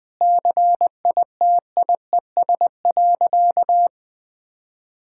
Morse code, heard as a keyed tone, about 21 wpm, 700 Hz. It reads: CITIES.
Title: Sentences with 5-Word Phrases - Spoken Beginning - 20wpm